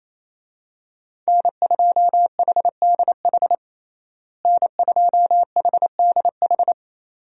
Morse code, heard as a keyed tone, about 28 wpm, 700 Hz. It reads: N2HDH N2HDH
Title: USA Call Signs - 5 Characters - Repeated 2x - 28wpm